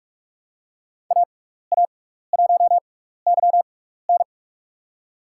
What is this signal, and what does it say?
Morse code, heard as a keyed tone, about 45 wpm, spaced as for 15 wpm, 700 Hz. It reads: AA1YN
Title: USA Call Signs - 6x Character Spacing - 45wpm